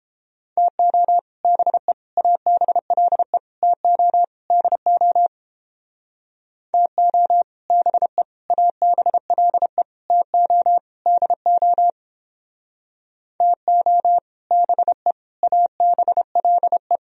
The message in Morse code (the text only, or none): TO BE ABLE TO DO TO BE ABLE TO DO TO BE ABLE